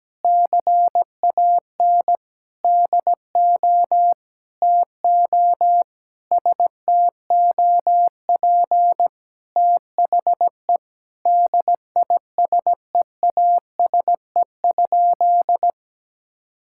Morse code, heard as a keyed tone, about 17 wpm, 700 Hz. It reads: CAN DO TO STOP THE DISEASE?
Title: Sentences from Top 1000 Words - Review Part 6 - 17wpm